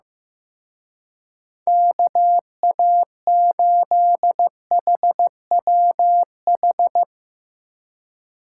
Morse code, heard as a keyed tone, about 16 wpm, 700 Hz. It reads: KA8HWH
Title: Speed Racing - USA 2x3 Call Signs 15wpm